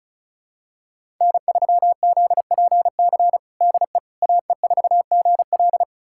Morse code, heard as a keyed tone, about 36 wpm, 700 Hz. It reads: N3ZPC DE AE4GL
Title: USA Callsign DE Callsign - 5 Characters - 35wpm